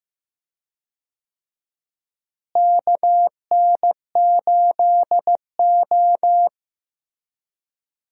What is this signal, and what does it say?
Morse code, heard as a keyed tone, about 15 wpm, 700 Hz. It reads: KN8O
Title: USA Call Signs - 4 Characters - Repeated 2x - 15wpm